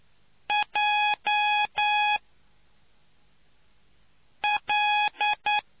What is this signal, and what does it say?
Morse code, heard as a keyed tone, about 10 wpm, 840 Hz. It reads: J L